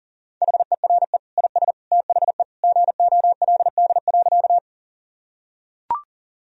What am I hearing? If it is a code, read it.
Morse code, heard as a keyed tone, about 40 wpm, 700 Hz. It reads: HERE IS THE GOLD.
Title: Sentences from Top 1000 Words - Review Part 3 - 40wpm